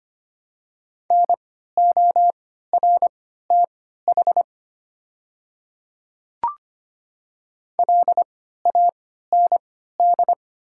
Morse code, heard as a keyed tone, about 25 wpm, spaced as for 14 wpm, 690 Hz. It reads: NORTH LAND
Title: 3x Character Spacing - Top 500 Words - 25wpm